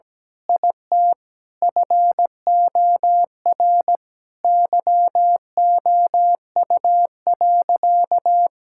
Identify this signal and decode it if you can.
Morse code, heard as a keyed tone, about 17 wpm, 700 Hz. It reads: IT FOR YOU.
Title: Sentences from Top 1000 Words - Review Part 6 - 17wpm